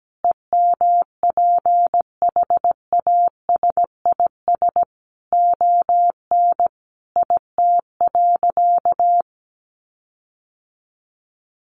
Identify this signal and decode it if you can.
Morse code, heard as a keyed tone, about 17 wpm, 700 Hz. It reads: EMPHASIS ON IT.
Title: Sentences from Top 1400 Words 17wpm